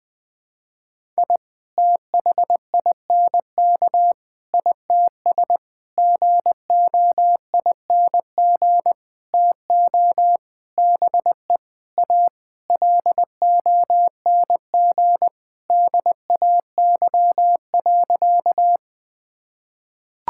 Morse code, heard as a keyed tone, about 20 wpm, 700 Hz. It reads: I THINK ITS GOING TO BE A LONG DAY.
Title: Sentences with 7-Word Phrases - 20wpm